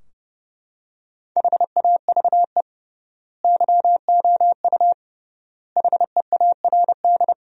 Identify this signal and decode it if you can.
Morse code, heard as a keyed tone, about 32 wpm, 700 Hz. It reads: HAVE YOU HEARD
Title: 3x Word-Spacing - Sentences from Top 400 Words - 30wpm